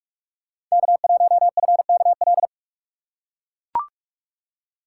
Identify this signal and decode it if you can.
Morse code, heard as a keyed tone, about 44 wpm, 700 Hz. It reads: K1FKL